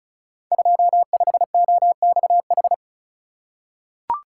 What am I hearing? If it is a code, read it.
Morse code, heard as a keyed tone, about 36 wpm, 700 Hz. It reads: 25OXH
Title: Code Groups - 5 Letter:Numbers - 35wpm